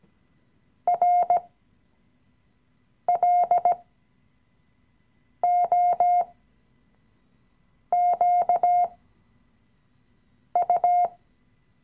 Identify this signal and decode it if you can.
Morse code, heard as a keyed tone, about 17 wpm, spaced as for 5 wpm, 710 Hz. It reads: RLOQU
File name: CW-Online-Kurs / Anfänger / Lektion 017